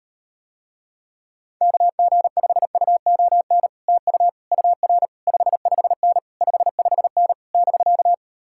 Morse code, heard as a keyed tone, about 38 wpm, 700 Hz. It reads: KG5UON TU UR 55N 55N <BK>